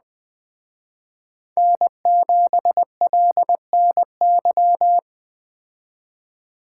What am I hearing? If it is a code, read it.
Morse code, heard as a keyed tone, about 20 wpm, 700 Hz. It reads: N7LNY